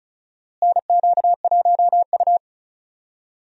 Morse code, heard as a keyed tone, about 34 wpm, 700 Hz. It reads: NQ1U
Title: USA Call Signs - 4 Characters - Repeated 2x - 35wpm